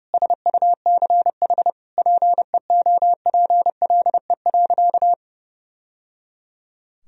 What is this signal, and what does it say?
Morse code, heard as a keyed tone, about 30 wpm, 700 Hz. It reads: SUCH PEOPLE.
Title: Sentences from Top 1000 Words - Review Part 3 - 30wpm